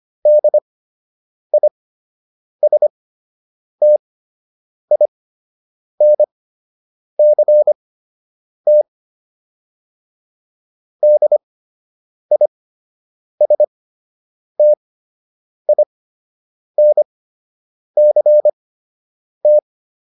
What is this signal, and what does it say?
Morse code, heard as a keyed tone, about 25 wpm, spaced as for 8 wpm, 600 Hz. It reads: DISTINCT DISTINCT